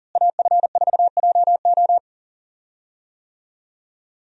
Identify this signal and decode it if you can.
Morse code, heard as a keyed tone, about 40 wpm, 700 Hz. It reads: AF4JO
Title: USA Call Signs - 5 Characters - Repeated 3x - 40wpm